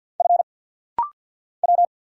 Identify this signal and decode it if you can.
Morse code, heard as a keyed tone, about 50 wpm, 700 Hz. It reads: F W
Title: Single Letter-Number - Rapid-Fire 50wpm